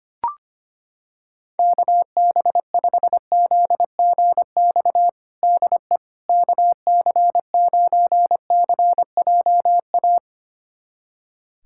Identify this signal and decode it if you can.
Morse code, heard as a keyed tone, about 25 wpm, 700 Hz. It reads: KB5ZGX DE KC9CJA
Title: USA Callsign DE Callsign - 6 Characters - 25wpm